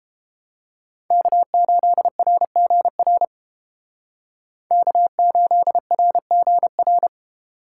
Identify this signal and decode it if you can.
Morse code, heard as a keyed tone, about 31 wpm, 700 Hz. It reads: K8RGR K8RGR